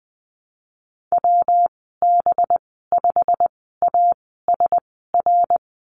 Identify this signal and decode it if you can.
Morse code, heard as a keyed tone, about 19 wpm, spaced as for 14 wpm, 700 Hz. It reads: WB5ASR
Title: 200 Lesson 32g - Letter k - Call Signs 20wpm